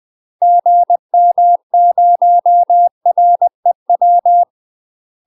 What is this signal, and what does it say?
Morse code, heard as a keyed tone, about 20 wpm, 700 Hz. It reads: GM0REW
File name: UK MASTER QSOs TRAINING / 20WPM / UK MASTER QSOs 0050